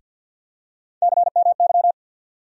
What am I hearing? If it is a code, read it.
Morse code, heard as a keyed tone, about 53 wpm, 700 Hz. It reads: CMY
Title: USA Call Sign Postfixes - 3 Letters 50wpm